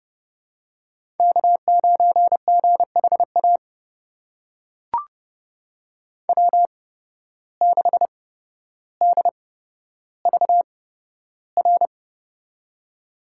Morse code, heard as a keyed tone, about 30 wpm, spaced as for 13 wpm, 700 Hz. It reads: K9GHA W6DVR
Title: USA Call Signs - 5 Characters - 8x Character Spacing - 30wpm